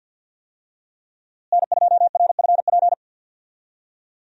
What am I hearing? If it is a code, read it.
Morse code, heard as a keyed tone, about 47 wpm, 700 Hz. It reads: N2RFP
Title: USA Call Signs - 5 Characters - Repeated 3x - 50wpm